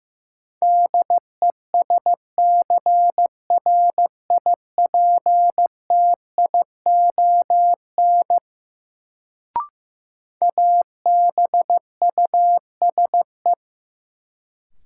Morse code, heard as a keyed tone, about 15 wpm, 700 Hz. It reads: DESCRIPTION ABUSE
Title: Sentences from Top 1300 Words - New Word Drill - 15wpm